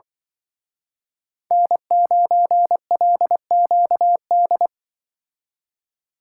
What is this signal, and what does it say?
Morse code, heard as a keyed tone, about 24 wpm, 700 Hz. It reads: N9LQD
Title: Speed Racing - USA 1x3 Call Signs 20wpm